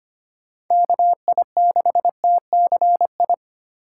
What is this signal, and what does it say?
Morse code, heard as a keyed tone, about 25 wpm, 700 Hz. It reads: KI6TCI
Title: USA Call Signs - 6 Characters - 4x Character Spacing - 25wpm